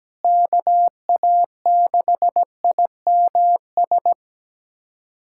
Morse code, heard as a keyed tone, about 17 wpm, 700 Hz. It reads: KA6IMS
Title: USA Callsign DE Callsign - 6 Characters - 17wpm